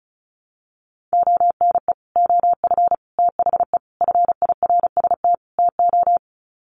Morse code, heard as a keyed tone, about 34 wpm, 700 Hz. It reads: ONE OF THE FIRST TO
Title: Sets of 5 Words Encore 35wpm